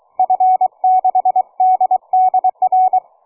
Morse code, heard as a keyed tone, about 23 wpm, 740 Hz. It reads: F6DDR